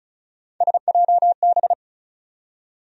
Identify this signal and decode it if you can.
Morse code, heard as a keyed tone, about 34 wpm, 700 Hz. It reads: SJB
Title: USA Call Sign Postfixes - 3 Letters 35wpm